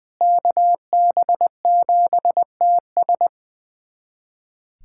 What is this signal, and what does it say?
Morse code, heard as a keyed tone, about 21 wpm, 700 Hz. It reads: KB7TS
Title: USA Callsign DE Callsign - 5 Characters - 20wpm